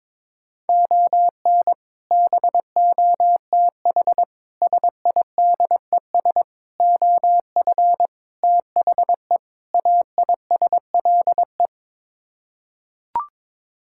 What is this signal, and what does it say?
Morse code, heard as a keyed tone, about 22 wpm, 700 Hz. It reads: ON BOTH SIDES OF THE AISLE
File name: practice / 22wpm / Sets of 6 Words - 22wpm